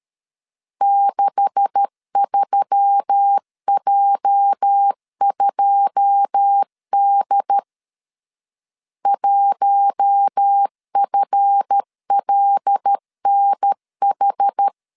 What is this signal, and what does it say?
Morse code, heard as a keyed tone, about 13 wpm, 800 Hz. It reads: 63J2D 1FLNH
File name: LEZIONI / 10 SIGNS / 10D COMA 12wpm 2544